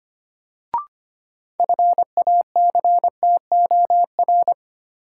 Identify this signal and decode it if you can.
Morse code, heard as a keyed tone, about 25 wpm, 700 Hz. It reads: FACTOR
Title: Sentences from Top 1300 Words - New Word Drill Alt - 25wpm